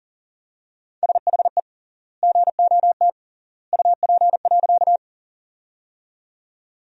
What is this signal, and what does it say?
Morse code, heard as a keyed tone, about 40 wpm, 700 Hz. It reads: SHE GOT UP.
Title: 3x Word-Spacing - Sentences from Top 300 Words - 40wpm